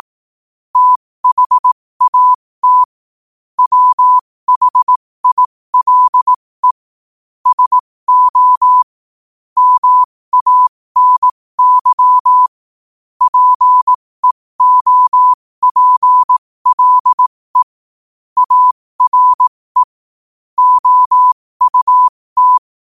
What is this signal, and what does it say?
Morse code, heard as a keyed tone, about 18 wpm, spaced as for 15 wpm, 1000 Hz. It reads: THAT WHILE SO MANY PEOPLE ARE OUT
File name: qotdfiles / 15wpm